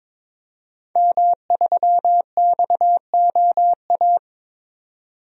Morse code, heard as a keyed tone, about 22 wpm, 700 Hz. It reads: M3XOA
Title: DXCC-100 Call Signs - 5 Characters - 22wpm